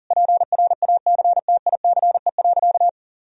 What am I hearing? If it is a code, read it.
Morse code, heard as a keyed tone, about 40 wpm, 700 Hz. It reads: PRACTICE.
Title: Sentences from Top 1000 Words - Review Part 4 - 40wpm